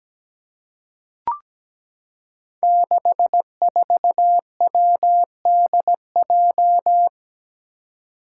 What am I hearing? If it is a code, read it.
Morse code, heard as a keyed tone, about 17 wpm, 700 Hz. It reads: 64WDJ
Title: Five Challenging Letter-Number Combinations 17wpm